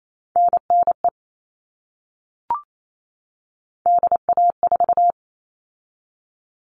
Morse code, heard as a keyed tone, about 28 wpm, 700 Hz. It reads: NNE DA4
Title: All Cut Numbers - 3-Digit 28wpm